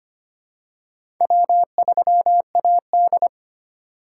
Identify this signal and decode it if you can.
Morse code, heard as a keyed tone, about 25 wpm, 700 Hz. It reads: W3AD
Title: USA Call Signs - 2x Character Spacing - 25wpm